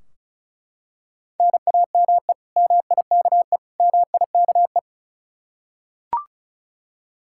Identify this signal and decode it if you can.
Morse code, heard as a keyed tone, about 36 wpm, 700 Hz. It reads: NAME MIKE MIKE
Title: Practice QSO - Exchange 1 - Beginning - 35wpm